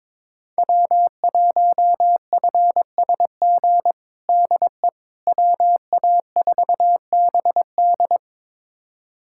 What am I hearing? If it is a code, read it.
Morse code, heard as a keyed tone, about 22 wpm, 700 Hz. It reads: W1FSG DE WA4BD
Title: USA Callsign DE Callsign - 5 Characters - 22wpm